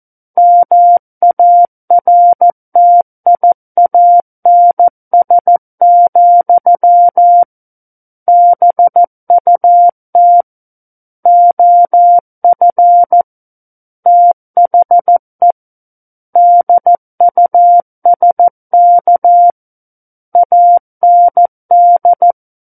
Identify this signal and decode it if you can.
Morse code, heard as a keyed tone, about 14 wpm, 700 Hz. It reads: MARTIANS, BUT OF THE DUSK AND
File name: Book One / 05-Chapter 5 - 14 WPM